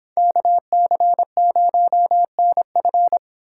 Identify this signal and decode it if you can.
Morse code, heard as a keyed tone, about 26 wpm, 700 Hz. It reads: KC0NF